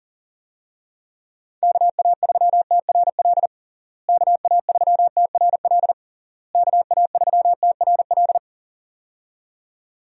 Morse code, heard as a keyed tone, about 40 wpm, 700 Hz. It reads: KA3TRL KA3TRL KA3TRL
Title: USA Call Signs - 6 Characters - Repeated 3x - 40wpm